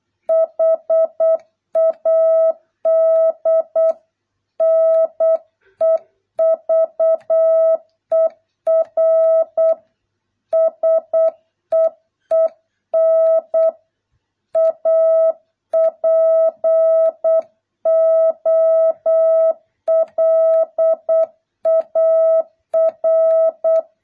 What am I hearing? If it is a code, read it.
Morse code, heard as a keyed tone, about 8 wpm, 650 Hz. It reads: HAD NEVER SEEN A POLAR